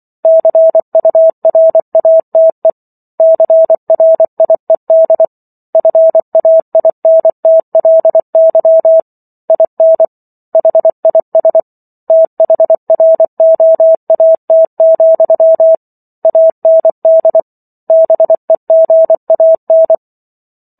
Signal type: Morse code